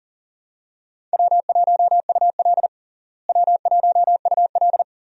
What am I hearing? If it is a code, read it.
Morse code, heard as a keyed tone, about 40 wpm, 700 Hz. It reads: W1UL W1UL
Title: USA Call Signs - 4 Characters - Repeated 2x - 40wpm